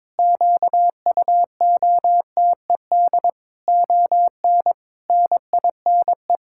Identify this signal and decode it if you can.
Morse code, heard as a keyed tone, about 21 wpm, 700 Hz. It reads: QUOTED ON NINE
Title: Pangram Challenge v2 - 22wpm